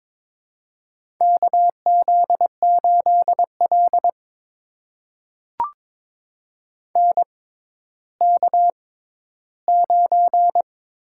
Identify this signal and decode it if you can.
Morse code, heard as a keyed tone, about 23 wpm, spaced as for 13 wpm, 700 Hz. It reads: KZ8L NK9
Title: USA Call Signs - 4 Characters - 6x Character Spacing - 22wpm